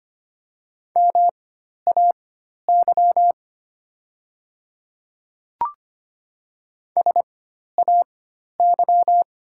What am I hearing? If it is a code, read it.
Morse code, heard as a keyed tone, about 25 wpm, spaced as for 12 wpm, 700 Hz. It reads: MAY SAY
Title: 4x Character Spacing - Top 500 Words - 25wpm